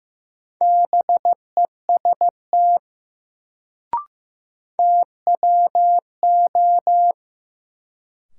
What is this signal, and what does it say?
Morse code, heard as a keyed tone, about 15 wpm, 700 Hz. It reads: BEST TWO
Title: Sets of 2 Words - New Word Drill - 15wpm